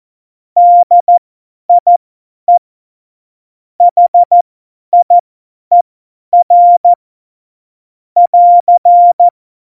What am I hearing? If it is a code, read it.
Morse code, heard as a keyed tone, about 14 wpm, spaced as for 10 wpm, 700 Hz. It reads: DIE HIER <AR>
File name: Morsekurs / Witze / Witz 14 April 2020, 50 F70